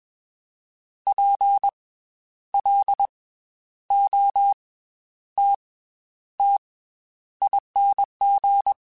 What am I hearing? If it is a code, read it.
Morse code, heard as a keyed tone, about 21 wpm, 800 Hz. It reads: P L O T T ING
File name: Special INGs / Special INGs-mp3s / Slow ing @20